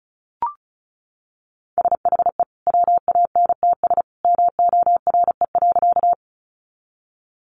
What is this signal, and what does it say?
Morse code, heard as a keyed tone, about 34 wpm, 700 Hz. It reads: SHE WANTS MORE.
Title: Easy Phrases 35wpm